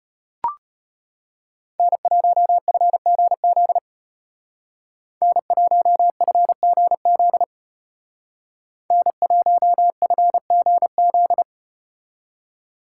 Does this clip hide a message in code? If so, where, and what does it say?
Yes: Morse code from 1.8 to 11.4 s, heard as a keyed tone, about 34 wpm, 700 Hz. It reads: N1FGZ N1FGZ N1FGZ